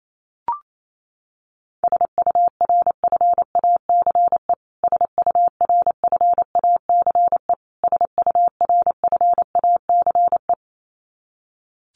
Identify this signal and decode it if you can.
Morse code, heard as a keyed tone, about 28 wpm, 700 Hz. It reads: SURFACE SURFACE SURFACE
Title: Top 500 Words Repeated 28wpm